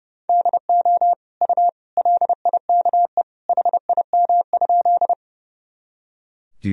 Morse code, heard as a keyed tone, about 30 wpm, 700 Hz. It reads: DO U LIKE HIM?
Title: Sentences from Top 100 Words and Abbreviations 30wpm